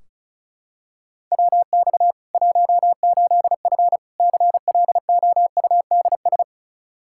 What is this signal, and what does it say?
Morse code, heard as a keyed tone, about 34 wpm, 700 Hz. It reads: WX 18F CLOUDS
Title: Practice QSO - Exchange 2 - Info - 35wpm